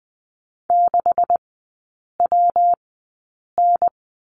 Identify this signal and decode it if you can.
Morse code, heard as a keyed tone, about 20 wpm, spaced as for 8 wpm, 690 Hz. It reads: 6WN
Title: 198 Lesson 32e - Letter k - Three Characters 20wpm